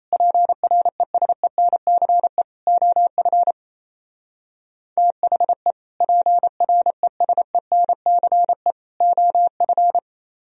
Morse code, heard as a keyed tone, about 32 wpm, 700 Hz. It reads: PRESENCE OF THE PRESENCE OF
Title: Speed-Racing - Sets of 3 Words 28wpm